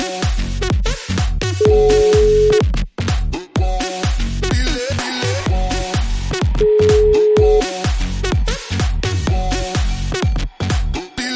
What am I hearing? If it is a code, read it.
Ringback tone (Europe, 425 Hz).